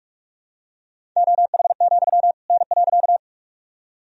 Morse code, heard as a keyed tone, about 41 wpm, 690 Hz. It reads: OH, N.
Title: Sentences from Top 500 Words and Abbreviations 45wpm